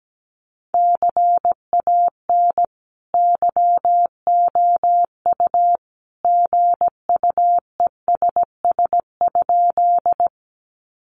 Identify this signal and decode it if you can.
Morse code, heard as a keyed tone, about 17 wpm, 700 Hz. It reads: CAN YOU GUESS?
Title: Sentences from Top 1500 Words 17wpm